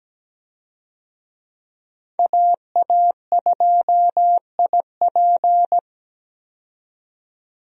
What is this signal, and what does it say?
Morse code, heard as a keyed tone, about 17 wpm, 700 Hz. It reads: AA2IP
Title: USA Call Signs - 5 Characters - 17wpm